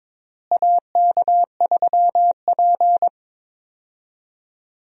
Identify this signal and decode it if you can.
Morse code, heard as a keyed tone, about 22 wpm, 700 Hz. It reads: AK3P
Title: USA Call Signs - 4 Characters - Repeated 2x - 22wpm